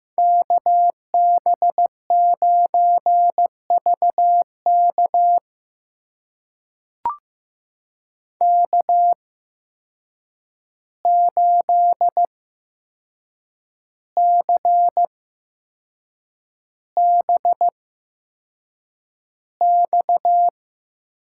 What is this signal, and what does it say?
Morse code, heard as a keyed tone, about 15 wpm, spaced as for 6 wpm, 700 Hz. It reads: KB9VK K8CBX
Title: USA Call Signs - 5 Characters - 8x Character Spacing - 15wpm